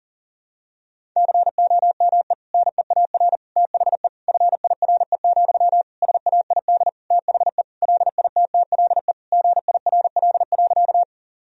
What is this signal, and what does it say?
Morse code, heard as a keyed tone, about 40 wpm, 700 Hz. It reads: COME NEAR THE FIRE, SAID THE LITTLE GIRL.
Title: Sentences from Top 1000 Words - Review Part 1 - 40wpm